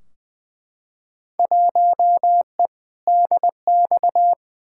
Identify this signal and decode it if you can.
Morse code, heard as a keyed tone, about 20 wpm, 700 Hz. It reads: 1E DX
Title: Field Day Exchanges Encore - 20wpm